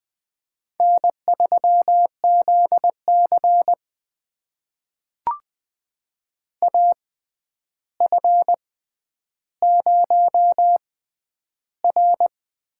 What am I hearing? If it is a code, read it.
Morse code, heard as a keyed tone, about 20 wpm, spaced as for 10 wpm, 700 Hz. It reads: N3ZC AF0R
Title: USA Call Signs - 4 Characters - 6x Character Spacing - 20wpm